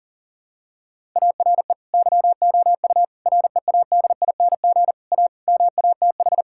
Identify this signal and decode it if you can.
Morse code, heard as a keyed tone, about 40 wpm, 700 Hz. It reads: ARE YOU READING A MATH
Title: Sentences from Top 1000 Words - Review Part 6 - 40wpm